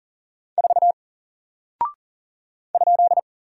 Morse code, heard as a keyed tone, about 40 wpm, 700 Hz. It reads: 4 ?
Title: Single Character (Letter:Number:Punctuation:Prosign) Alt 40wpm